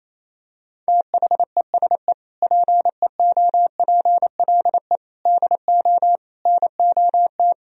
Morse code, heard as a keyed tone, about 28 wpm, 700 Hz. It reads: THESE PEOPLE DO NOT
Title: Sentences from Top 1000 Words - Review Part 1 - 28wpm